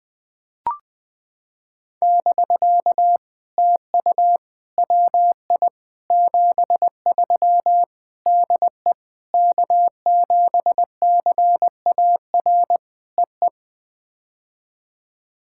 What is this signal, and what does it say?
Morse code, heard as a keyed tone, about 20 wpm, 700 Hz. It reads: <BK> TU WI 73 DE K7CAR EE
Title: POTA QSO Practice - 20wpm